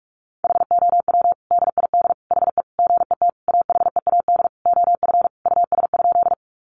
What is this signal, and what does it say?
Morse code, heard as a keyed tone, about 46 wpm, 700 Hz. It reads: HOW DID HE GET AHEAD OF US?
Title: Sentences from Top 1100 Words 45wpm